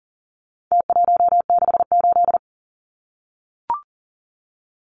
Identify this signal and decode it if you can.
Morse code, heard as a keyed tone, about 40 wpm, 700 Hz. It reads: T168
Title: Numbers - Contest Style - 4-Digit 40wpm